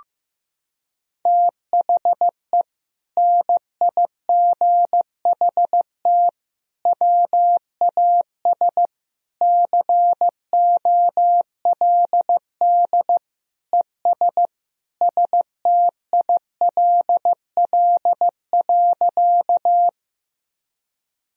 Morse code, heard as a keyed tone, about 15 wpm, 700 Hz. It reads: THE NIGHT WAS COLD ES STILL.
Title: Sentences from Top 400 Words and Abbreviations 15wpm